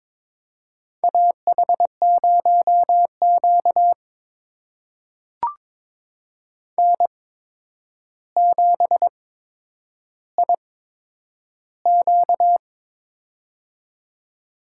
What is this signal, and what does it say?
Morse code, heard as a keyed tone, about 22 wpm, spaced as for 9 wpm, 700 Hz. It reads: AH0Q N7IQ
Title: USA Call Signs - 4 Characters - 8x Character Spacing - 22wpm